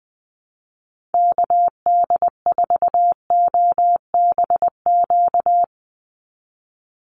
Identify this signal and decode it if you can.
Morse code, heard as a keyed tone, about 20 wpm, 700 Hz. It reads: KD4OBQ
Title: USA Call Signs 20wpm